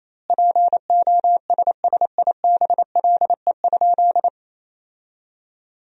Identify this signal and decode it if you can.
Morse code, heard as a keyed tone, about 28 wpm, 700 Hz. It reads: POSSIBLE?
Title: 2x Word-Spacing - Sentences from Top 500 Words - 28wpm